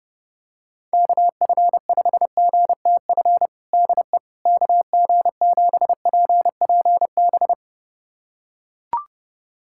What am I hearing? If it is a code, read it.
Morse code, heard as a keyed tone, about 30 wpm, 700 Hz. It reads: KF5GTF DE KG7PPB
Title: USA Callsign DE Callsign 30wpm